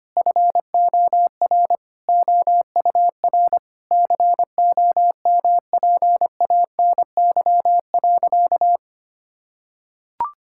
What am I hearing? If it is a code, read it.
Morse code, heard as a keyed tone, about 25 wpm, 700 Hz. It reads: FOR OUR COMPANY.
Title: Sentences from Top 1000 Words - Review Part 5 - 25wpm